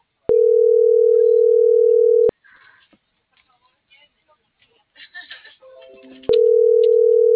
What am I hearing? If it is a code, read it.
Ringback tone (North America), 440 + 480 Hz.